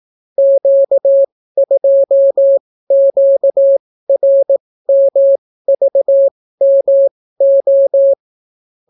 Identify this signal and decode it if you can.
Morse code, heard as a keyed tone, about 18 wpm, spaced as for 14 wpm, 550 Hz. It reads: Q2QRMVMO